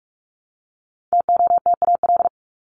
Morse code, heard as a keyed tone, about 44 wpm, 700 Hz. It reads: TOTAL
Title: Top 1251-1500 Words 45wpm